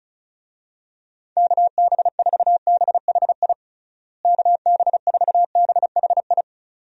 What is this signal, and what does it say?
Morse code, heard as a keyed tone, about 36 wpm, 700 Hz. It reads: KB4BHI KB4BHI
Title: USA Call Signs - 6 Characters - Repeated 2x - 35wpm